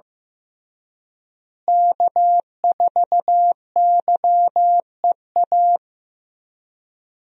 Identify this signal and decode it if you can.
Morse code, heard as a keyed tone, about 15 wpm, 700 Hz. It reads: K4YEA
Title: USA Call Signs - 5 Characters - Repeated 3x - 15wpm